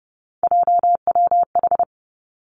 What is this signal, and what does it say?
Morse code, heard as a keyed tone, about 30 wpm, 700 Hz. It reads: JWH
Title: Three Challenging Letter-Number Combinations 30wpm